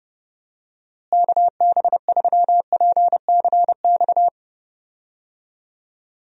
Morse code, heard as a keyed tone, about 30 wpm, 700 Hz. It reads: KB3PCX
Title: USA Call Signs - 6 Characters - Repeated 3x - 30wpm